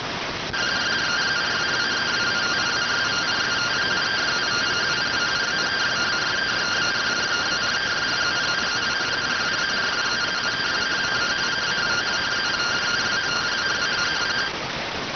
Two-tone teleprinter signal, about 170 Hz shift.